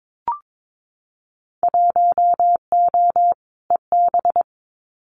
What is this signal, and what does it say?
Morse code, heard as a keyed tone, about 22 wpm, 700 Hz. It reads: 1O EB
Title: Winter Field Day Exchanges 22wpm